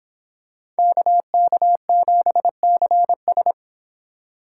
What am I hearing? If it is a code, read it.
Morse code, heard as a keyed tone, about 25 wpm, 700 Hz. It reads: KK7CS